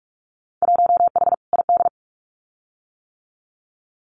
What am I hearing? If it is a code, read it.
Morse code, heard as a keyed tone, about 46 wpm, 700 Hz. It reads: 1H ID
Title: Winter Field Day Exchanges 45wpm